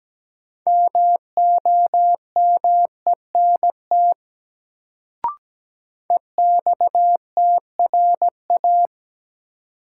Morse code, heard as a keyed tone, about 17 wpm, 700 Hz. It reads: MOMENT EXTRA
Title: Sentences from Top 1000 Words - New Word Drill - 17wpm